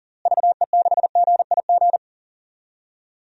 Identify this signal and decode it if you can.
Morse code, heard as a keyed tone, about 40 wpm, 700 Hz. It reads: VE6GIG
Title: DXCC-100 Call Signs - 6 Characters - Repeated 3x - 40wpm